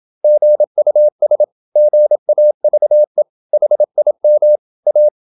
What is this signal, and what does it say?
Morse code, heard as a keyed tone, about 27 wpm, 600 Hz. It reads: GUS GAVE HIM A
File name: Never 0027wpm